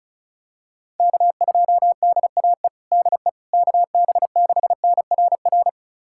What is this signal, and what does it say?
Morse code, heard as a keyed tone, about 34 wpm, 700 Hz. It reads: K2DAE DE KB6NRR
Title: USA Callsign DE Callsign 35wpm